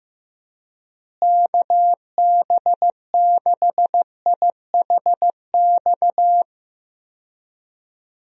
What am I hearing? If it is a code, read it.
Morse code, heard as a keyed tone, about 16 wpm, 700 Hz. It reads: KB6IHX
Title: USA Call Signs - 6 Characters - Repeated 3x - 15wpm